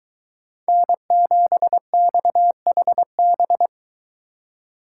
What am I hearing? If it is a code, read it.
Morse code, heard as a keyed tone, about 22 wpm, 700 Hz. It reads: N7XHB